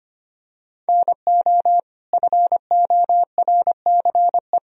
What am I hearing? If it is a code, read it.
Morse code, heard as a keyed tone, about 25 wpm, 700 Hz. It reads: NO FORCE
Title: Sentences from Top 1000 Words - Review Part 3 - 25wpm